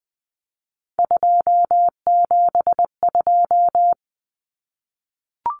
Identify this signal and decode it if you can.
Morse code, heard as a keyed tone, about 20 wpm, 700 Hz. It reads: 272
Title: Numbers - 3-Digit 20wpm